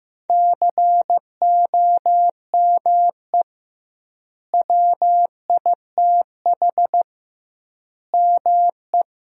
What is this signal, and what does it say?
Morse code, heard as a keyed tone, about 15 wpm, 700 Hz. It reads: COME WITH ME
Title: 2x Word-Spacing - Sentences from Top 200 Words - 15wpm